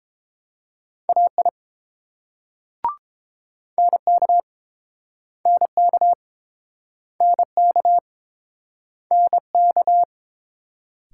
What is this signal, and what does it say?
Morse code, heard as a keyed tone, about 30 wpm, 700 Hz. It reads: AI NK NK NK NK